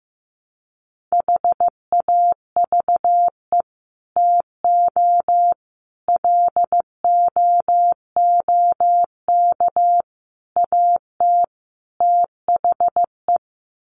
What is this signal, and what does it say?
Morse code, heard as a keyed tone, about 15 wpm, 700 Hz. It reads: HAVE TO LOOK AT THE
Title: Sets of 5 Words Encore 15wpm